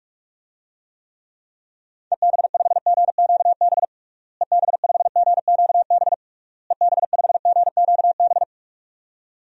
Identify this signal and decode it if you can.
Morse code, heard as a keyed tone, about 43 wpm, 700 Hz. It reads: EB5GQB EB5GQB EB5GQB
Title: DXCC-100 Call Signs - 6 Characters - Repeated 3x - 45wpm